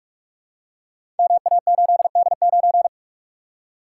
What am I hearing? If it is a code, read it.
Morse code, heard as a keyed tone, about 44 wpm, 700 Hz. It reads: MA8D9
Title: Code Groups - 5 Letter:Numbers - 45wpm